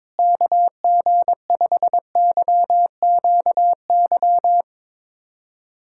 Morse code, heard as a keyed tone, about 22 wpm, 700 Hz. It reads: KG5YQY
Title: USA Callsign DE Callsign - 6 Characters - 22wpm